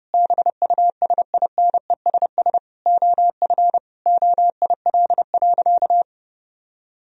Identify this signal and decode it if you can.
Morse code, heard as a keyed tone, about 32 wpm, 700 Hz. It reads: BUSINESS OF OIL.
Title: Sentences from Top 1000 Words - Review Part 4 - 30wpm